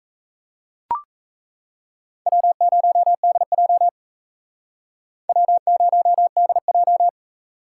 Morse code, heard as a keyed tone, about 35 wpm, 700 Hz. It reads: W0DJ W0DJ